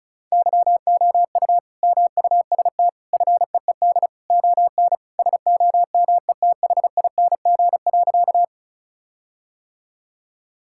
Morse code, heard as a keyed tone, about 36 wpm, 700 Hz. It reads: YOU MUST FEED ON SOMETHING.
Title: Sentences from Top 1600 Words 35wpm